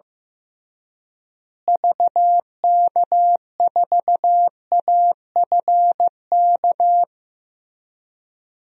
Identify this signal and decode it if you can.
Morse code, heard as a keyed tone, about 15 wpm, 700 Hz. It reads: VK4AFK
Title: DXCC-100 Call Signs - 6 Characters - Repeated 3x - 15wpm